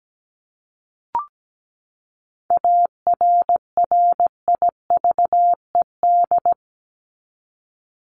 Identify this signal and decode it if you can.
Morse code, heard as a keyed tone, about 17 wpm, 700 Hz. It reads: ARRIVED
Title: ED Suffix - Top 100 Words with Base Words 17wpm